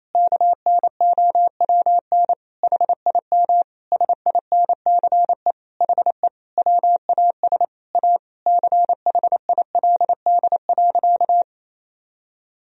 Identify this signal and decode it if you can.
Morse code, heard as a keyed tone, about 28 wpm, 700 Hz. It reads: KNOWN HIM SINCE HE WAS A CHILD.
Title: Sentences from Top 1000 Words - Review Part 3 - 28wpm